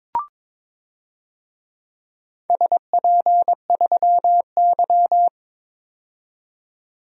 Morse code, heard as a keyed tone, about 22 wpm, 700 Hz. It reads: SP3Y
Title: DXCC-100 Call Signs - 4 Characters - 22wpm